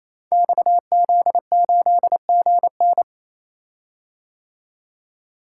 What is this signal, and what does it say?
Morse code, heard as a keyed tone, about 28 wpm, 700 Hz. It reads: XZ8GN
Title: Code Groups - 5 Letter:Numbers - 28wpm